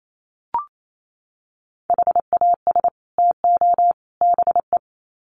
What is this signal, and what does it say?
Morse code, heard as a keyed tone, about 28 wpm, 700 Hz. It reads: HAS TO BE
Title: Sets of 3 Words 28wpm